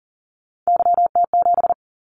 Morse code, heard as a keyed tone, about 40 wpm, 700 Hz. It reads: YT7
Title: Callsign Prefixes for Top 25 DXCC - 40wpm